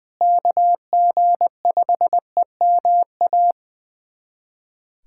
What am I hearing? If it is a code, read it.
Morse code, heard as a keyed tone, about 20 wpm, 700 Hz. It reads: KG5EMA
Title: USA Callsign DE Callsign - 6 Characters - 20wpm